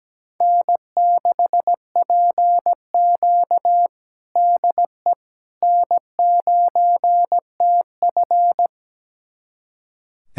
Morse code, heard as a keyed tone, about 17 wpm, 700 Hz. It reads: N6PQ DE N9TF
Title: USA Callsign DE Callsign - 4 Characters - 17wpm